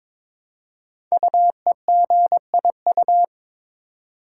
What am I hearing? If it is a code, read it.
Morse code, heard as a keyed tone, about 22 wpm, 700 Hz. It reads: UEGIU
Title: Code Groups - 5 Letters - 22wpm